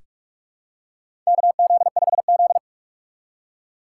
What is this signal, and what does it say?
Morse code, heard as a keyed tone, about 46 wpm, 700 Hz. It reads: KZ5Z
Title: K1USN SST QSO Practice - 45wpm